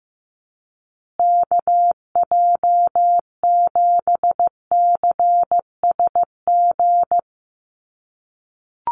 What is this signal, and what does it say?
Morse code, heard as a keyed tone, about 15 wpm, 700 Hz. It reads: KJ7CSG
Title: USA Call Signs Repeated 2x 15wpm